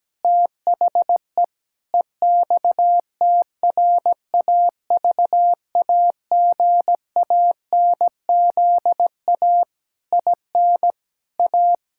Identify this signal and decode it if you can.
Morse code, heard as a keyed tone, about 17 wpm, 700 Hz. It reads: THE EXTRAVAGANZA IN A
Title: Pangram Challenge v2 - 17wpm